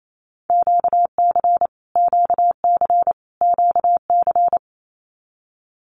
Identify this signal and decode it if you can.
Morse code, heard as a keyed tone, about 28 wpm, 700 Hz. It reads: QC QC QC
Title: Canadian Provinces and Territories - Repeated 3x - 28wpm